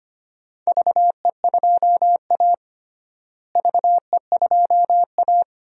Morse code, heard as a keyed tone, about 25 wpm, 700 Hz. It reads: VE2A VE2A
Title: DXCC-100 Call Signs - 4 Characters - Repeated 3x - 25wpm